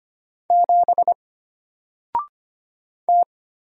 Morse code, heard as a keyed tone, about 25 wpm, 700 Hz. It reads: 7 T
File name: Single-Letter-Number-25wpm